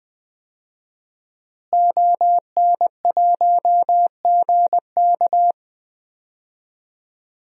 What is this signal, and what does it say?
Morse code, heard as a keyed tone, about 20 wpm, 700 Hz. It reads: ON1GK
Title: DXCC-100 Call Signs - 5 Characters - Repeated 2x - 20wpm